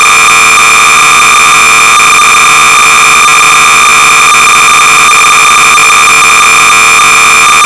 Cassette data (Kansas City type).
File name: sorcerer / 300baud / Space-(author and date unknow)-BASIC-Sorcerer-300baud